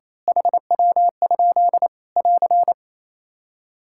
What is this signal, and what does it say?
Morse code, heard as a keyed tone, about 28 wpm, 700 Hz. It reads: HW? <AR>
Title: Practice QSO - Exchange 3 - More Info - 28wpm